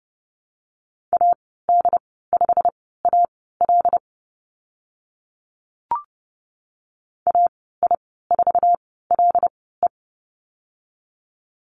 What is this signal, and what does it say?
Morse code, heard as a keyed tone, about 30 wpm, spaced as for 17 wpm, 700 Hz. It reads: AD5AL AI4LE
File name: Lesson 15g - Number 5 - Call Signs 30wpm